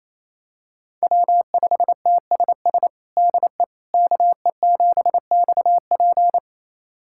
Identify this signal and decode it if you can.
Morse code, heard as a keyed tone, about 28 wpm, 700 Hz. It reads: W5TSS DE KE7XP
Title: USA Callsign DE Callsign 28wpm